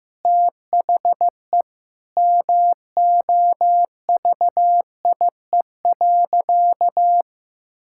Morse code, heard as a keyed tone, about 16 wpm, 700 Hz. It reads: THE MOVIE.
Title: Sentences from Top 1000 Words - Review Part 5 - 15wpm